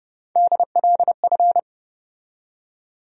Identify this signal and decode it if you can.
Morse code, heard as a keyed tone, about 30 wpm, 700 Hz. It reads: DLF
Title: USA Call Sign Postfixes - 3 Letters 30wpm